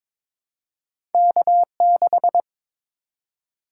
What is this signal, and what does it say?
Morse code, heard as a keyed tone, about 22 wpm, 700 Hz. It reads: K6
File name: practice / 22wpm / Two Letter-Number Combinations - 22wpm